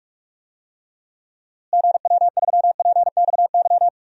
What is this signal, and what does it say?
Morse code, heard as a keyed tone, about 42 wpm, 700 Hz. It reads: GW3PXY
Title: DXCC-100 Call Signs - 6 Characters - Repeated 3x - 45wpm